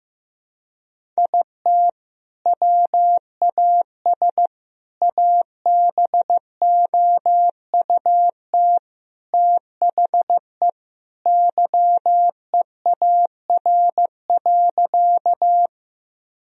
Morse code, heard as a keyed tone, about 15 wpm, 700 Hz. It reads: IT WAS ABOUT THE YEAR.
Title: Sentences from Top 1000 Words - Review Part 1 - 15wpm